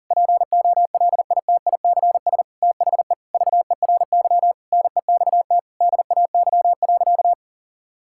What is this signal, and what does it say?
Morse code, heard as a keyed tone, about 40 wpm, 700 Hz. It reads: POLITICS THE VERY NEXT DAY.
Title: Sentences from Top 1000 Words - Review Part 6 - 40wpm